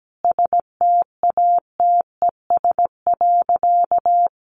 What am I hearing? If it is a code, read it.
Morse code, heard as a keyed tone, about 17 wpm, 700 Hz. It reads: STATES.
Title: A Sound Of Thunder 17wpm